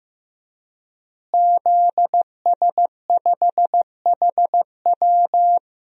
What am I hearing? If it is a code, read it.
Morse code, heard as a keyed tone, about 15 wpm, 700 Hz. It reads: ZS5HW
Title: DXCC-100 Call Signs - 5 Characters - Repeated 3x - 15wpm